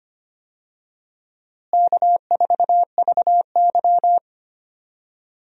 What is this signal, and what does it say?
Morse code, heard as a keyed tone, about 25 wpm, 700 Hz. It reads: K4VY
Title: USA Call Signs - 4 Characters - Repeated 2x - 25wpm